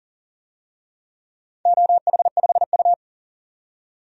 Morse code, heard as a keyed tone, about 41 wpm, 700 Hz. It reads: OH5U
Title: DXCC-100 Call Signs - 4 Characters - 40wpm